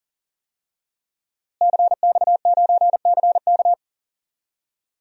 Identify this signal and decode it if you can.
Morse code, heard as a keyed tone, about 39 wpm, 700 Hz. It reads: CX9CK
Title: DXCC-100 Call Signs - 5 Characters - Repeated 2x - 40wpm